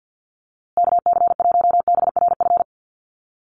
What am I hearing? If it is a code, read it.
Morse code, heard as a keyed tone, about 50 wpm, 700 Hz. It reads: KC1BRF
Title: USA Call Signs Repeated 2x 50wpm